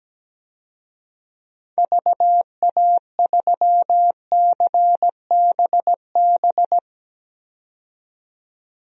Morse code, heard as a keyed tone, about 17 wpm, 700 Hz. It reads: VA3CBB
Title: DXCC-100 Call Signs - 6 Characters - 17wpm